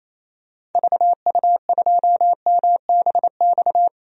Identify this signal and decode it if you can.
Morse code, heard as a keyed tone, about 28 wpm, 700 Hz. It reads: VU2MBX
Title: DXCC-100 Call Signs - 6 Characters - 28wpm